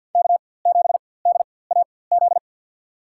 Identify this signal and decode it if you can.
Morse code, heard as a keyed tone, about 47 wpm, spaced as for 23 wpm, 700 Hz. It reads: K7DAZ
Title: USA Call Signs - 5 Characters - 4x Character Spacing - 50wpm